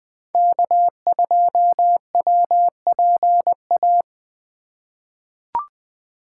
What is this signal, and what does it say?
Morse code, heard as a keyed tone, about 21 wpm, 700 Hz. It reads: K2WPA
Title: 2023 POTA Activator Call Signs - 20wpm